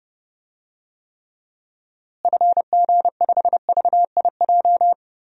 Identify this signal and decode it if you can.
Morse code, heard as a keyed tone, about 30 wpm, 700 Hz. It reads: FG5VIJ
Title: DXCC-100 Call Signs - 6 Characters - Repeated 2x - 30wpm